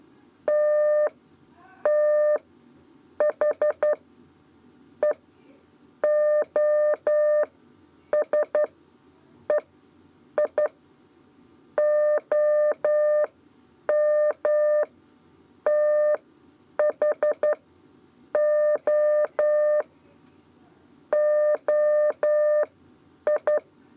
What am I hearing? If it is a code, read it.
Morse code, heard as a keyed tone, about 11 wpm, spaced as for 8 wpm, 600 Hz. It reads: T T H E OS E I OM TH O OI
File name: Lezione trasmessa 2 lez 2 exe